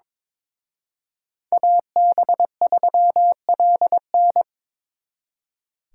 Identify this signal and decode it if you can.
Morse code, heard as a keyed tone, about 26 wpm, 700 Hz. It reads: AB3LN
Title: Speed Racing - USA 2x2 Call Signs 22wpm